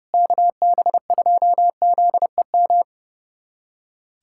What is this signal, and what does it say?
Morse code, heard as a keyed tone, about 32 wpm, 700 Hz. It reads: KB2ZEM
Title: USA Callsign DE Callsign - 6 Characters - 30wpm